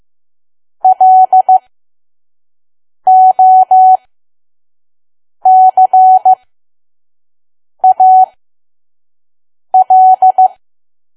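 Morse code, heard as a keyed tone, about 15 wpm, spaced as for 5 wpm, 750 Hz. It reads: LOCAL
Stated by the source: Morse code practice at 5 WPM from KQ2P.COM.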